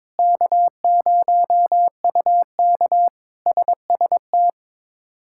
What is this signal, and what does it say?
Morse code, heard as a keyed tone, about 22 wpm, 700 Hz. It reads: K0UK SST
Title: K1USN SST QSO Practice - 22wpm